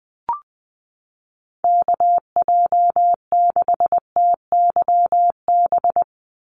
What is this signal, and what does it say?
Morse code, heard as a keyed tone, about 20 wpm, 700 Hz. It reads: KJ6TYB
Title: USA Call Signs 20wpm